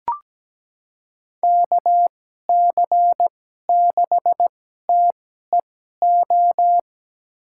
Morse code, heard as a keyed tone, about 17 wpm, spaced as for 12 wpm, 700 Hz. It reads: KC6TEO
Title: USA Call Signs - 6 Characters - 2x Character Spacing - 17wpm